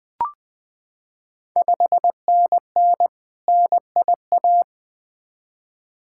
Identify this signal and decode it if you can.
Morse code, heard as a keyed tone, about 20 wpm, 700 Hz. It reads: 5NN NIA
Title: Ontario QSO Party Exchange 20wpm